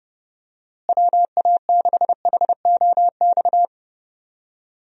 Morse code, heard as a keyed tone, about 32 wpm, 700 Hz. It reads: WA6HOX
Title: USA Call Signs - 6 Characters - Repeated 2x - 30wpm